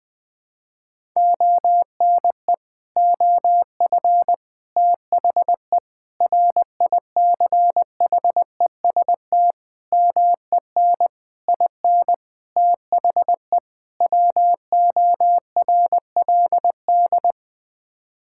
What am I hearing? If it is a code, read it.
Morse code, heard as a keyed tone, about 20 wpm, 700 Hz. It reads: ONE OF THE RICHEST MEN IN THE WORLD
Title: Sets of 8 Words - 20wpm